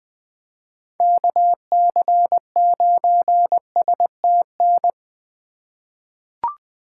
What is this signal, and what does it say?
Morse code, heard as a keyed tone, about 20 wpm, 700 Hz. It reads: KC9STN
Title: USA Call Signs - 6 Characters - 3x Character Spacing - 20wpm